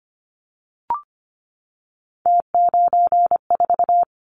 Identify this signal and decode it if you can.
Morse code, heard as a keyed tone, about 25 wpm, 700 Hz. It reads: T94
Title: Callsign Prefixes for Top 50 DXCC - 25wpm